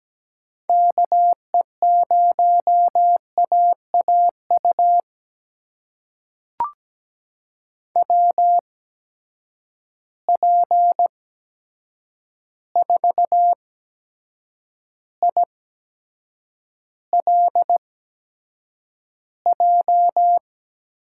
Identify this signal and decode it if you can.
Morse code, heard as a keyed tone, about 17 wpm, spaced as for 7 wpm, 700 Hz. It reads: KE0AAU WP4ILJ
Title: USA Call Signs - 6 Characters - 8x Character Spacing - 17wpm